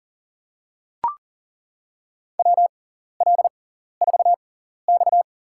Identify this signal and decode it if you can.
Morse code, heard as a keyed tone, about 40 wpm, spaced as for 14 wpm, 700 Hz. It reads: WL4X
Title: USA Call Signs - 4 Characters - 6x Character Spacing - 40wpm